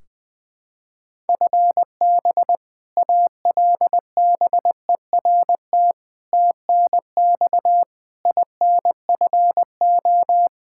Morse code, heard as a keyed tone, about 21 wpm, 700 Hz. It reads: FB ALBERT TNX INFO
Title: Practice QSO - Exchange 3 - More Info - 20wpm